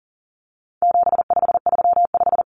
Morse code, heard as a keyed tone, about 40 wpm, 700 Hz. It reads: 7535
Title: Common Cut Numbers - 4-Digit 40wpm